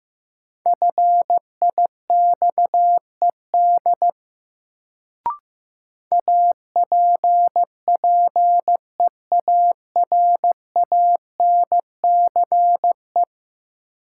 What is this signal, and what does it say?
Morse code, heard as a keyed tone, about 15 wpm, 700 Hz. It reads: FIXED APPEARANCE
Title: Sentences from Top 1100 Words - New Word Drill Alt - 15wpm